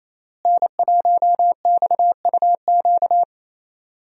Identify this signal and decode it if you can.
Morse code, heard as a keyed tone, about 28 wpm, 700 Hz. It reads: N1XUQ
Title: USA Call Signs - 5 Characters - Repeated 3x - 28wpm